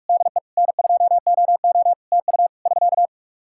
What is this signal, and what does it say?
Morse code, heard as a keyed tone, about 46 wpm, 700 Hz. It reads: DE N2OO TU <SK>
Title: Practice QSO - Exchange 4 - Ending - 45wpm